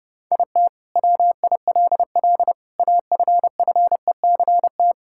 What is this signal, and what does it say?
Morse code, heard as a keyed tone, about 30 wpm, 700 Hz. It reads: IT WILL AFFECT
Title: Sentences from Top 1000 Words - Review Part 6 - 30wpm